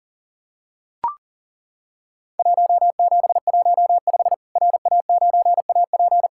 Morse code, heard as a keyed tone, about 40 wpm, 700 Hz. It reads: 1715 RA9AP
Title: Worked All Europe DX Contest - QTC Exchange - 40wpm